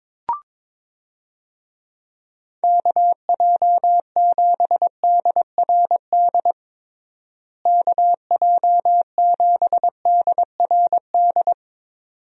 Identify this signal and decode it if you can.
Morse code, heard as a keyed tone, about 22 wpm, 700 Hz. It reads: KJ7DRD KJ7DRD
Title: USA Call Signs - 6 Characters - Repeated 2x - 22wpm